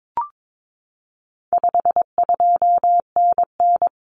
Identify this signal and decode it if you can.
Morse code, heard as a keyed tone, about 22 wpm, 700 Hz. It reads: 52NN
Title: Common Cut Numbers - 4-Digit 22wpm